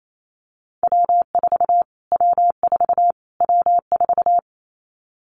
Morse code, heard as a keyed tone, about 28 wpm, 700 Hz. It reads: W4 W4 W4
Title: Callsign Prefixes for Top 25 DXCC - Repeated 3x - 28wpm